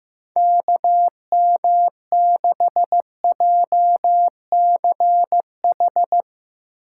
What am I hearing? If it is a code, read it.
Morse code, heard as a keyed tone, about 16 wpm, 700 Hz. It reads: KM6JCH
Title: USA Call Signs - 6 Characters - 4x Character Spacing - 15wpm